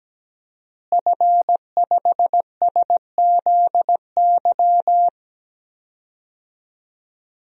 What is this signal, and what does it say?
Morse code, heard as a keyed tone, about 17 wpm, 700 Hz. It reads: F5SZY
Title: DXCC-100 Call Signs - 5 Characters - Repeated 3x - 17wpm